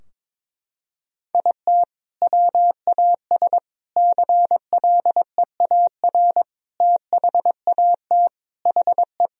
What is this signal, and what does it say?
Morse code, heard as a keyed tone, about 22 wpm, 700 Hz. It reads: IT WAS CLEAR THAT HE
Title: Sentences from Top 1000 Words - Review Part 4 - 22wpm